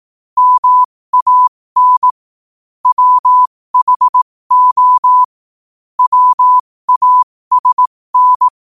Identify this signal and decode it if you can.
Morse code, heard as a keyed tone, about 18 wpm, spaced as for 15 wpm, 1000 Hz. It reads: MAN WHO WASN
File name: qotdfiles / 15wpm